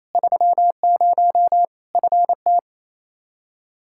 Morse code, heard as a keyed tone, about 28 wpm, 700 Hz. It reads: 30 FT
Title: Practice QSO - Exchange 2 - Info - 28wpm